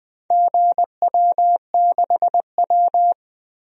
Morse code, heard as a keyed tone, about 20 wpm, 700 Hz. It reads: GW6W
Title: DXCC-100 Call Signs - 4 Characters - Repeated 2x - 20wpm